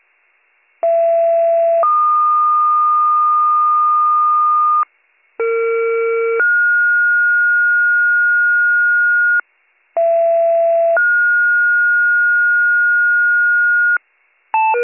Two-tone page, heard as a one-second tone, then a three-second tone, about 670 then 1220 Hz, about 460 then 1430 Hz, about 670 then 1430 Hz, with noise. It is heard from 0.8 to 14.0 s.